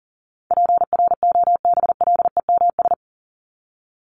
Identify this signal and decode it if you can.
Morse code, heard as a keyed tone, about 40 wpm, 700 Hz. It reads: PROBLEMS
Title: ES:S Suffix - Top 100 Words with Base Words 40wpm